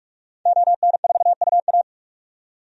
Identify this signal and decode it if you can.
Morse code, heard as a keyed tone, about 46 wpm, 700 Hz. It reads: ON4UA
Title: DXCC-100 Call Signs - 5 Characters - 45wpm